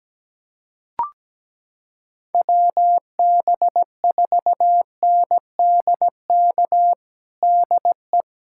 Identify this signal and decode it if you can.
Morse code, heard as a keyed tone, about 17 wpm, 700 Hz. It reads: WB4NDK DE
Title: USA Callsign DE Callsign - 6 Characters - 17wpm